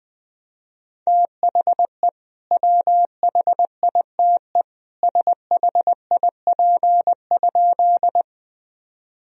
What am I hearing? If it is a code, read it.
Morse code, heard as a keyed tone, about 20 wpm, 700 Hz. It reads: THE WHITE SHIP?
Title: Sentences from Top 1000 Words - Review Part 2 - 20wpm